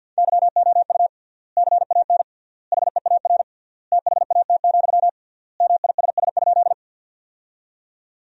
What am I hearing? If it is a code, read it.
Morse code, heard as a keyed tone, about 53 wpm, 700 Hz. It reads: YOU CAN HEAR THAT, MISS?
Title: 3x Word-Spacing - Sentences from Top 500 Words - 50wpm